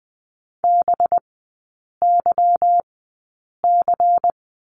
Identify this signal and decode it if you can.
Morse code, heard as a keyed tone, about 20 wpm, spaced as for 8 wpm, 700 Hz. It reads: BYC
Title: 184 Lesson 30e - Letter b - Three Characters 20wpm